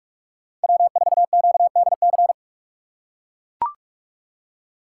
Morse code, heard as a keyed tone, about 44 wpm, 700 Hz. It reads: W4QDC